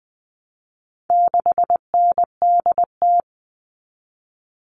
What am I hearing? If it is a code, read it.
Morse code, heard as a keyed tone, about 20 wpm, 700 Hz. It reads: 6NDT
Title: All Cut Numbers - 4-Digit 20wpm